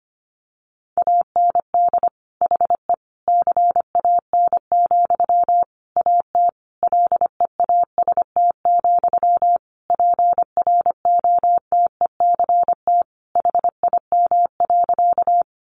Morse code, heard as a keyed tone, about 25 wpm, 700 Hz. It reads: AND HE CAN, AT LEAST, PROTECT HIM.
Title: Sentences from Top 900 Words 25wpm